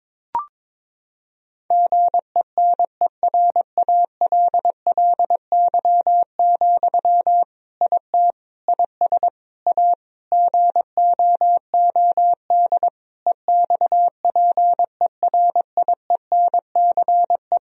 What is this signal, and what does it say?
Morse code, heard as a keyed tone, about 22 wpm, 700 Hz. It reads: GENERALLY, IT IS A GOOD EXPERIENCE
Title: Sentences from Top 1000 Words - Review Part 4 - 22wpm